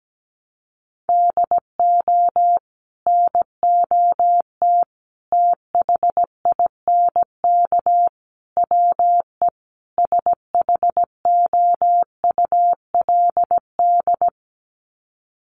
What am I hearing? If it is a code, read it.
Morse code, heard as a keyed tone, about 17 wpm, 700 Hz. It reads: DO NOT THINK WE SHOULD
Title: Sets of 5 Words Encore 17wpm